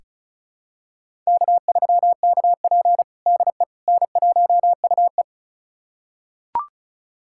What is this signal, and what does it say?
Morse code, heard as a keyed tone, about 36 wpm, 700 Hz. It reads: K3KP DE N1UE
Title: USA Callsign DE Callsign - 4 Characters - 35wpm